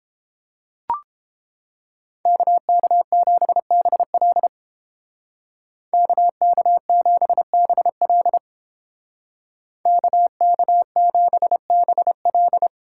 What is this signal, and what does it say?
Morse code, heard as a keyed tone, about 30 wpm, 700 Hz. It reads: KK7BL KK7BL KK7BL